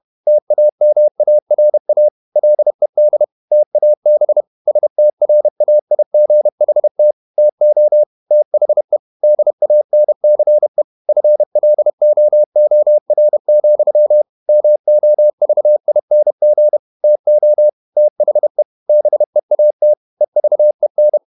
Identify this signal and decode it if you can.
Morse code, heard as a keyed tone, about 31 wpm, 600 Hz. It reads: TAMARA LED TAB STRAIGHT TO THE DANCE FLOOR, MOVING TO THE BEAT EVEN
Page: Never 31wpm | CW med Gnister